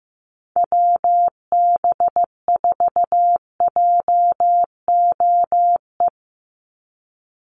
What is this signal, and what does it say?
Morse code, heard as a keyed tone, about 15 wpm, 700 Hz. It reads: WB4JOE
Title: USA Call Signs 15wpm